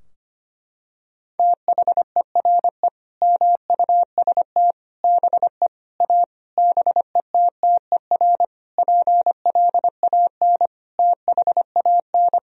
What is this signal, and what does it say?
Morse code, heard as a keyed tone, about 25 wpm, 700 Hz. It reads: THERE MUST BE A BETTER PLAN THAN
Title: Sentences from Top 1000 Words - Review Part 4 - 25wpm